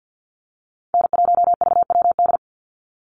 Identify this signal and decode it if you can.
Morse code, heard as a keyed tone, about 50 wpm, 700 Hz. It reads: N1VWD
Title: NA SOTA Chaser Callsigns 50wpm